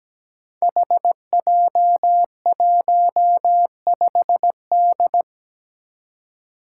Morse code, heard as a keyed tone, about 17 wpm, 700 Hz. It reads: HJ15D
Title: Five Challenging Letter-Number Combinations 17wpm